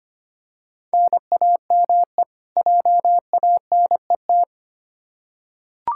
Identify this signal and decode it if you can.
Morse code, heard as a keyed tone, about 25 wpm, 700 Hz. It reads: NAME JANET
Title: K1USN SST QSO Practice 25wpm